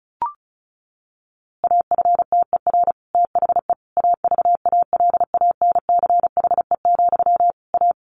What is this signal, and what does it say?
Morse code, heard as a keyed tone, about 34 wpm, 700 Hz. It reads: AFTER THE AVALANCHE, A
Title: A Sound Of Thunder 35wpm